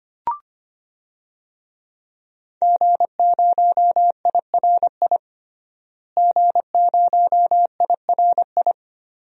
Morse code, heard as a keyed tone, about 25 wpm, 700 Hz. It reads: G0IRI G0IRI
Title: DXCC-100 Call Signs - 5 Characters - Repeated 3x - 25wpm